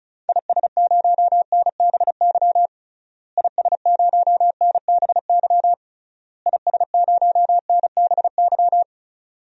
Morse code, heard as a keyed tone, about 36 wpm, 700 Hz. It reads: IS0NBY IS0NBY IS0NBY
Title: DXCC-100 Call Signs - 6 Characters - Repeated 3x - 35wpm